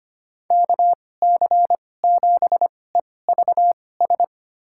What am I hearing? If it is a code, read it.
Morse code, heard as a keyed tone, about 25 wpm, spaced as for 18 wpm, 700 Hz. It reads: KC7EVS
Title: USA Call Signs - 6 Characters - 2x Character Spacing - 25wpm